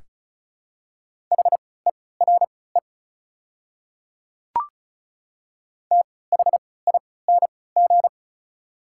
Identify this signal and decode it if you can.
Morse code, heard as a keyed tone, about 36 wpm, spaced as for 20 wpm, 700 Hz. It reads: HERE THING
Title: 3x Character Spacing - Top 300 Words - 35wpm